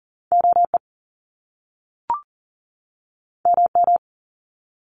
Morse code, heard as a keyed tone, about 40 wpm, 700 Hz. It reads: OE MM
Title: Top Two Letter Combinations in Words 40wpm